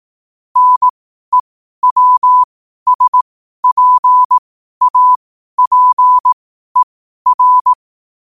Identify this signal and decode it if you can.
Morse code, heard as a keyed tone, about 18 wpm, spaced as for 12 wpm, 1000 Hz. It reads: NEWSPAPER
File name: qotdfiles / 12wpm